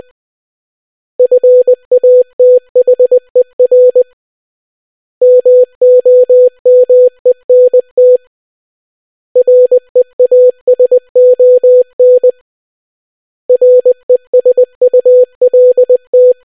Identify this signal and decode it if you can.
Morse code, heard as a keyed tone, about 20 wpm, 500 Hz. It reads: FATHER MOMENT REASON RESULT